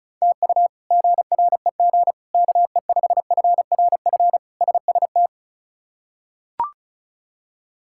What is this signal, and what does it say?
Morse code, heard as a keyed tone, about 36 wpm, 700 Hz. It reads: TU GREG KE5FRF SST
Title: K1USN SST QSO Practice - 35wpm